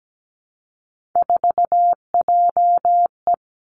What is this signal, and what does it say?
Morse code, heard as a keyed tone, about 17 wpm, 700 Hz. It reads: 4JE
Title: Three Challenging Letter-Number Combinations 17wpm